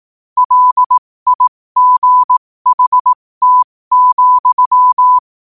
Morse code, heard as a keyed tone, about 18 wpm, spaced as for 15 wpm, 1000 Hz. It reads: LIGHT,
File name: qotdfiles / 15wpm